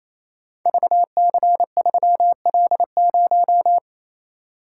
Morse code, heard as a keyed tone, about 29 wpm, 700 Hz. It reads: VC3L0
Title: Code Groups - 5 Letter:Numbers - 28wpm